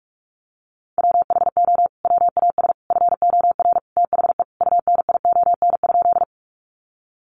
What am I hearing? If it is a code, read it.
Morse code, heard as a keyed tone, about 44 wpm, 700 Hz. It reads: WHO WAS FOR THE UNION?
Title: Sentences from Top 1400 Words 45wpm